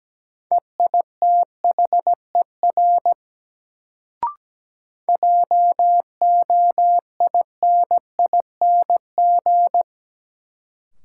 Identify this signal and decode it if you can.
Morse code, heard as a keyed tone, about 17 wpm, 700 Hz. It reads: EITHER JOINING
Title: Sets of 5 Words - New Word Drill - 17wpm